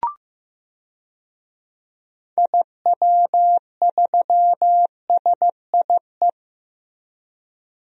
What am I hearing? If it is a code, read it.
Morse code, heard as a keyed tone, about 16 wpm, 700 Hz. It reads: IW3SIE
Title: DXCC-100 Call Signs - 6 Characters - Repeated 3x - 15wpm